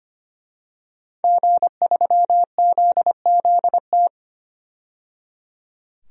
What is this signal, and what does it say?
Morse code, heard as a keyed tone, about 25 wpm, 700 Hz. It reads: G3ZZT
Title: DXCC-100 Call Signs - 5 Characters - 25wpm